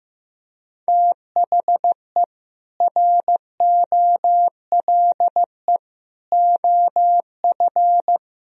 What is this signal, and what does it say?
Morse code, heard as a keyed tone, about 15 wpm, 700 Hz. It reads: THE ROLE OF
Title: Speed-Racing - Sets of 3 Words 15wpm